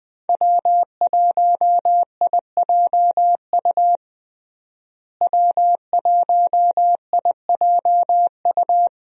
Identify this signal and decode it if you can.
Morse code, heard as a keyed tone, about 19 wpm, 700 Hz. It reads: W1IJU W1IJU
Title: USA Call Signs - 5 Characters - Repeated 3x - 20wpm